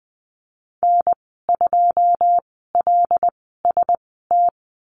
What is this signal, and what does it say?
Morse code, heard as a keyed tone, about 20 wpm, spaced as for 15 wpm, 700 Hz. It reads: N2LST
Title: 082 Lesson 14g - Number 2 - Call Signs 20wpm